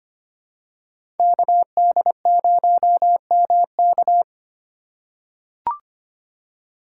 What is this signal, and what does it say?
Morse code, heard as a keyed tone, about 25 wpm, 700 Hz. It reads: KD0MK
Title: USA Call Signs - 2x Character Spacing - 25wpm